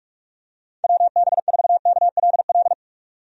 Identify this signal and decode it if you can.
Morse code, heard as a keyed tone, about 46 wpm, 700 Hz. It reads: WB4KLL
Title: USA Call Signs - 6 Characters - 45wpm